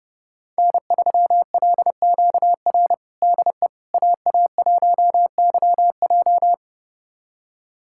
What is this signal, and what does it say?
Morse code, heard as a keyed tone, about 30 wpm, 700 Hz. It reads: N3LQR DE AA1YJ
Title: USA Callsign DE Callsign - 5 Characters - 30wpm